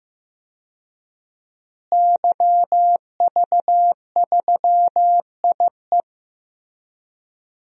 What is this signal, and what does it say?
Morse code, heard as a keyed tone, about 15 wpm, 700 Hz. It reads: YV3IE
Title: DXCC-100 Call Signs - 5 Characters - Repeated 2x - 15wpm